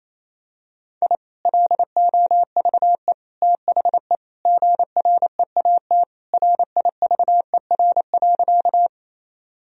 Morse code, heard as a keyed tone, about 28 wpm, 700 Hz. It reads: I LOVE THE GREAT RIVER.
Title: Sentences from Top 1000 Words - Review Part 1 - 28wpm